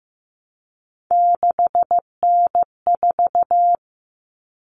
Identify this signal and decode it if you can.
Morse code, heard as a keyed tone, about 15 wpm, 700 Hz. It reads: 6N4
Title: Common Cut Numbers - 3-Digit 15wpm